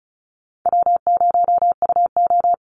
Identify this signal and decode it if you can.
Morse code, heard as a keyed tone, about 34 wpm, 700 Hz. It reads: W0UO
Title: Call Signs - 35wpm